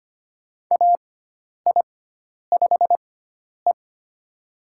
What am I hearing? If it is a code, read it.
Morse code, heard as a keyed tone, about 25 wpm, spaced as for 10 wpm, 700 Hz. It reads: AI5E
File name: Lesson 16g - Letter c - Call Signs 25wpm